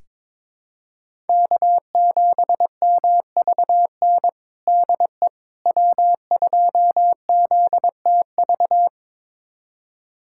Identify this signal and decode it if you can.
Morse code, heard as a keyed tone, about 22 wpm, 700 Hz. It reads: K7MVN DE W2ZTV
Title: USA Callsign DE Callsign 22wpm